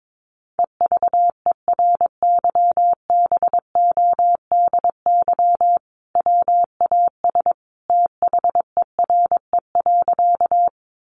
Morse code, heard as a keyed tone, about 22 wpm, 700 Hz. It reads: EVERYBODY WAS THERE.
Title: Sentences from Top 1400 Words 22wpm